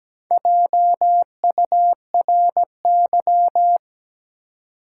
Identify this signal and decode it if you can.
Morse code, heard as a keyed tone, about 17 wpm, 700 Hz. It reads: JURY
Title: Four Letter Words - 17wpm